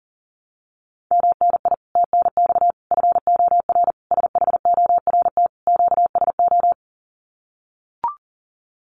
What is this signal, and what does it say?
Morse code, heard as a keyed tone, about 37 wpm, 700 Hz. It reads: MNI TNX FOR SHORT QSO
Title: Common QSO Phrases 40wpm